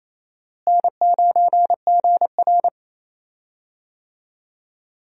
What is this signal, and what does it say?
Morse code, heard as a keyed tone, about 27 wpm, 700 Hz. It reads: N9GR
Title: USA Call Signs - 4 Characters - Repeated 3x - 28wpm